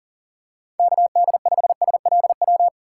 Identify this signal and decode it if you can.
Morse code, heard as a keyed tone, about 39 wpm, 700 Hz. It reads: KD5SLW
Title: USA Call Signs - 2x Character Spacing - 40wpm